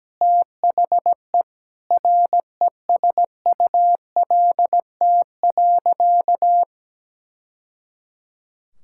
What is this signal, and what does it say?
Morse code, heard as a keyed tone, about 17 wpm, 700 Hz. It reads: THE RESULT.
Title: Sentences from Top 1000 Words - Review Part 5 - 17wpm